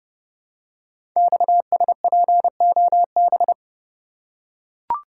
Morse code, heard as a keyed tone, about 30 wpm, 700 Hz. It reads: XSPOB
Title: Code Groups - 5 Letters - 30wpm